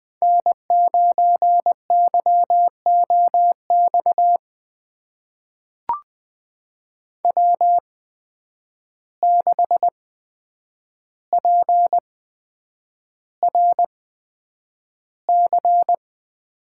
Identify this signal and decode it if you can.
Morse code, heard as a keyed tone, about 20 wpm, spaced as for 9 wpm, 700 Hz. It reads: N9YOX W6PRC
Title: USA Call Signs - 5 Characters - 8x Character Spacing - 20wpm